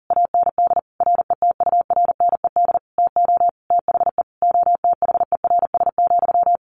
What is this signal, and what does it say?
Morse code, heard as a keyed tone, about 40 wpm, 700 Hz. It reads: AND RETURNED TO THE OTHERS,
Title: A Sound Of Thunder 40wpm